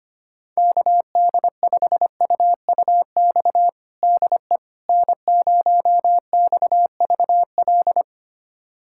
Morse code, heard as a keyed tone, about 25 wpm, 700 Hz. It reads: KD5UUX DE N0XVL
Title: USA Callsign DE Callsign 25wpm